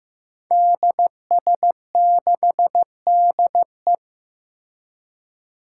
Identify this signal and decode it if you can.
Morse code, heard as a keyed tone, about 15 wpm, 700 Hz. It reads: DS6DE
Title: Five Challenging Letter-Number Combinations 15wpm